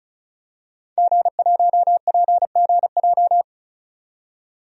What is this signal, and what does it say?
Morse code, heard as a keyed tone, about 36 wpm, 700 Hz. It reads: G1PGJ
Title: DXCC-100 Call Signs - 5 Characters - 35wpm